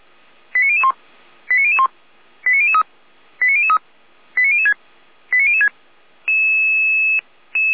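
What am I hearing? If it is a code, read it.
Five-tone selective call (ZVEI), heard as quick runs of five tones, tones of about 80 ms. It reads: 89001 89001 89003 89003 89006 89006
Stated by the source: ZVEI five-tone call received by SDR.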